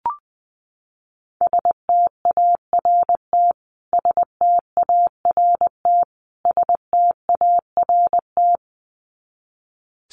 Morse code, heard as a keyed tone, about 20 wpm, 700 Hz. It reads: START START START
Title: Top 500 Words Repeated 20wpm